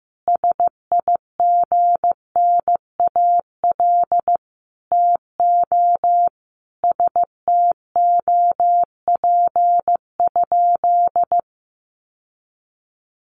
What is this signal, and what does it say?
Morse code, heard as a keyed tone, about 15 wpm, 700 Hz. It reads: SIGNAL TO STOP?
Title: Sentences from Top 1400 Words 15wpm